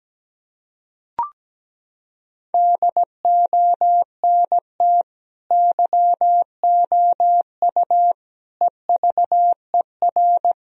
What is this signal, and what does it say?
Morse code, heard as a keyed tone, about 17 wpm, 700 Hz. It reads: DONT YOU EVER
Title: Sentences from Top 1000 Words - Review Part 1 - 17wpm